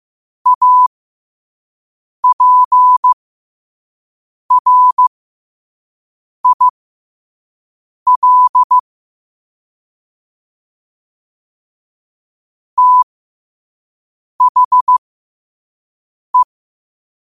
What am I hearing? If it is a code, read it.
Morse code, heard as a keyed tone, about 15 wpm, spaced as for 5 wpm, 1000 Hz. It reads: APRIL THE
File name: qotdfiles / 5wpm